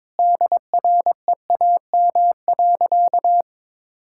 Morse code, heard as a keyed tone, about 21 wpm, 700 Hz. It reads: DREAM.
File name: practice / 22wpm / Sentences with 2-Word Phrases - 22wpm